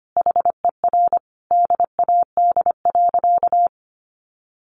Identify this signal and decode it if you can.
Morse code, heard as a keyed tone, about 25 wpm, 700 Hz. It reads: HER DAD.
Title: Additional Easy Phrases 25wpm